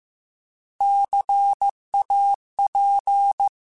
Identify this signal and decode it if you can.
Morse code, heard as a keyed tone, about 15 wpm, 800 Hz. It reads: CAP